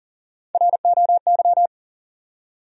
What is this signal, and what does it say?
Morse code, heard as a keyed tone, about 40 wpm, 700 Hz. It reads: ROY
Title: USA Common Names - Max 3 Letters - 40wpm